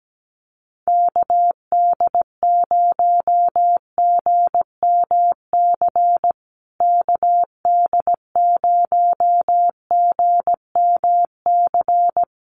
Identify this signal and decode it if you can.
Morse code, heard as a keyed tone, about 17 wpm, 700 Hz. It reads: KD0GMC KD0GMC
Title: USA Call Signs Repeated 3x 17wpm